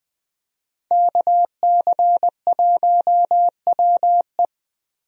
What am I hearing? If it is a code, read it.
Morse code, heard as a keyed tone, about 20 wpm, 700 Hz. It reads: KC1WE
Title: USA Call Signs - 5 Characters - Repeated 2x - 20wpm